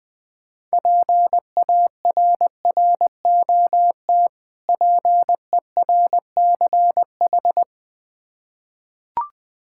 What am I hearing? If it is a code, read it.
Morse code, heard as a keyed tone, about 20 wpm, 700 Hz. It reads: PARROT PERCH
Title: Difficult Words with Hints - 20wpm